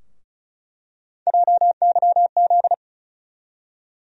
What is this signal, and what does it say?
Morse code, heard as a keyed tone, about 36 wpm, 700 Hz. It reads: JYZ
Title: USA Call Sign Postfixes - 3 Letters 35wpm